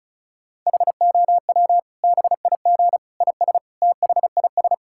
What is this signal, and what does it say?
Morse code, heard as a keyed tone, about 36 wpm, 700 Hz. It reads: HOW BIG IS THIS
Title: Sentences from Top 1600 Words 35wpm